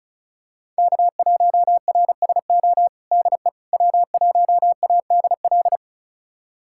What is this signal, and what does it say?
Morse code, heard as a keyed tone, about 36 wpm, 690 Hz. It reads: K1RSO DE W1ADL
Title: USA Callsign DE Callsign - 5 Characters - 35wpm